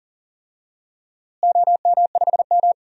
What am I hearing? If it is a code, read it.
Morse code, heard as a keyed tone, about 40 wpm, 700 Hz. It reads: OM5M
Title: DXCC-100 Call Signs - 4 Characters - Repeated 2x - 40wpm